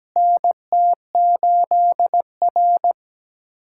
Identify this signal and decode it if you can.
Morse code, heard as a keyed tone, about 17 wpm, 700 Hz. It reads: NT8R
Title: USA Callsign DE Callsign - 4 Characters - 17wpm